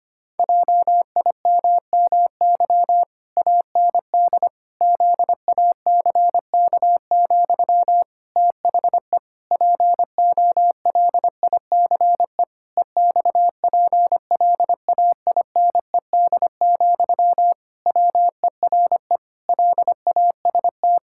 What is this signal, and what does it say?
Morse code, heard as a keyed tone, about 25 wpm, 700 Hz. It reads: JIMMY AND ZACK, THE POLICE EXPLAINED, WERE LAST
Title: Pangram Challenge v2 - 25wpm